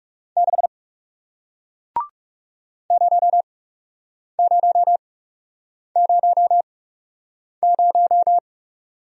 Morse code, heard as a keyed tone, about 41 wpm, spaced as for 8 wpm, 700 Hz. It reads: 6 0000